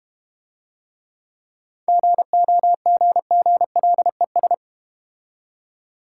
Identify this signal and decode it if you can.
Morse code, heard as a keyed tone, about 33 wpm, 700 Hz. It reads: GOGGLES